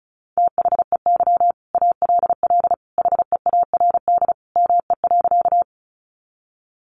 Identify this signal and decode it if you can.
Morse code, heard as a keyed tone, about 34 wpm, 700 Hz. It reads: THEY ALL HEARD ME.
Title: Sentences from Top 400 Words 35wpm